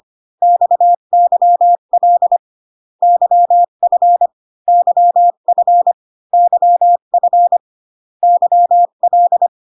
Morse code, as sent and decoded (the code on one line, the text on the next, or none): -..- -.-- .-.. / -.-- ..-. / -.-- ..-. / -.-- ..-. / -.-- .-..
XYL YF YF YF YL